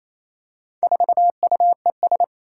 Morse code, heard as a keyed tone, about 27 wpm, 700 Hz. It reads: 4UES
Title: Four Challenging Letter-Number Combinations 28wpm